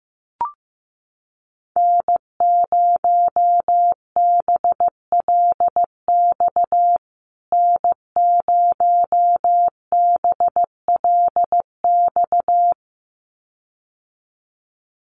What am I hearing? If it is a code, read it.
Morse code, heard as a keyed tone, about 15 wpm, 700 Hz. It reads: N0BLX N0BLX
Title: USA Call Signs Repeated 2x 15wpm